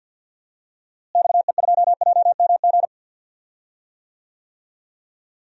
Morse code, heard as a keyed tone, about 47 wpm, 700 Hz. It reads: XE2JMG
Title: DXCC-100 Call Signs - 6 Characters - Repeated 2x - 50wpm